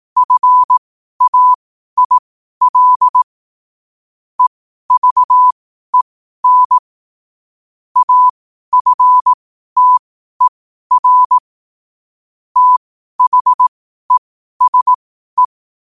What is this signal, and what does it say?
Morse code, heard as a keyed tone, about 18 wpm, spaced as for 12 wpm, 1000 Hz. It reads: FAIL EVEN AFTER THESE
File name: qotdfiles / 12wpm